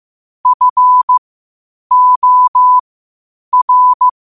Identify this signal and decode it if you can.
Morse code, heard as a keyed tone, about 15 wpm, spaced as for 8 wpm, 1000 Hz. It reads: FOR
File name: qotdfiles / 8wpm